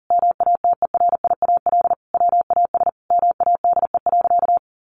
Morse code, heard as a keyed tone, about 40 wpm, 700 Hz. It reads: MATERIAL WAS MADE.
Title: Sentences from Top 1200 Words 40wpm